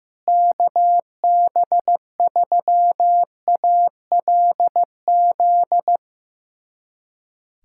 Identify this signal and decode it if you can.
Morse code, heard as a keyed tone, about 15 wpm, 700 Hz. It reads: KB3ALZ
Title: USA Callsign DE Callsign 15wpm